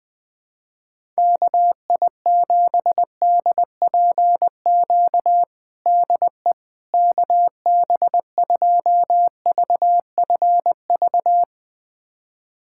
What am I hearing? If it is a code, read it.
Morse code, heard as a keyed tone, about 20 wpm, 700 Hz. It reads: KI7DPQ DE KB2VFV
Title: USA Callsign DE Callsign - 6 Characters - 20wpm